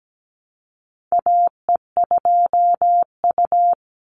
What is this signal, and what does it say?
Morse code, heard as a keyed tone, about 17 wpm, 700 Hz. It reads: AE2U
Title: Call Signs - 17wpm